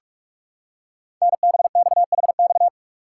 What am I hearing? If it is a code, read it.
Morse code, heard as a keyed tone, about 44 wpm, 690 Hz. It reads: NBXHX
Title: Code Groups - 5 Letters - 45wpm